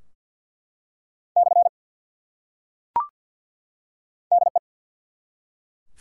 Morse code, heard as a keyed tone, about 47 wpm, 700 Hz. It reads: / DE
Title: Top 180 QSO Elements 50wpm